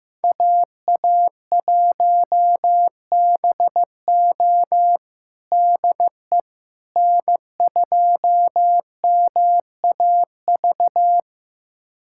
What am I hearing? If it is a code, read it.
Morse code, heard as a keyed tone, about 15 wpm, 700 Hz. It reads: AA1BO DE N2MAV
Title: USA Callsign DE Callsign - 5 Characters - 15wpm